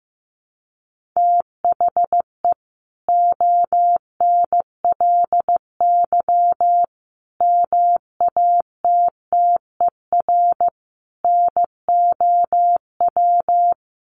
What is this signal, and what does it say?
Morse code, heard as a keyed tone, about 15 wpm, 700 Hz. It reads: THE ONLY MATTER NOW
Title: Sentences from Top 1400 Words 15wpm